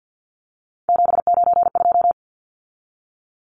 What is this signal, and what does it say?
Morse code, heard as a keyed tone, about 48 wpm, 700 Hz. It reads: 792
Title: Numbers - 3-Digit 50wpm